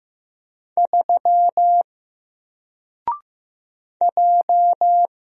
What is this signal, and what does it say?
Morse code, heard as a keyed tone, about 15 wpm, 700 Hz. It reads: 3 J
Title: Single Letter-Number Alt 15wpm